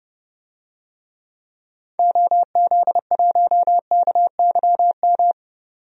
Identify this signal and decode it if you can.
Morse code, heard as a keyed tone, about 30 wpm, 700 Hz. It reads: OZ1KYM
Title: DXCC-100 Call Signs - 6 Characters - 30wpm